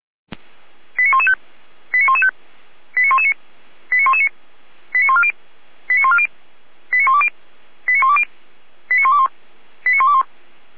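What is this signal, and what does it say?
Five-tone selective call (ZVEI), heard as quick runs of five tones, tones of about 70 ms. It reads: 89115 89115 89119 89119 89140 89140 89120 89120 89121 89121
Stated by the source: ZVEI five-tone call received by SDR.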